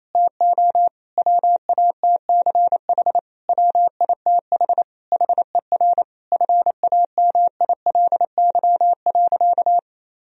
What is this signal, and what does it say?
Morse code, heard as a keyed tone, about 27 wpm, 700 Hz. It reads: TO WATCH WITH HER FAMILY.
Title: Sentences from Top 1000 Words - Review Part 4 - 28wpm